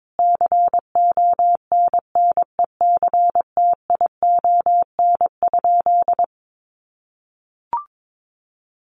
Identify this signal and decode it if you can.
Morse code, heard as a keyed tone, about 22 wpm, 700 Hz. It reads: CONNECTION?
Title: Sentences from Top 1300 Words 22wpm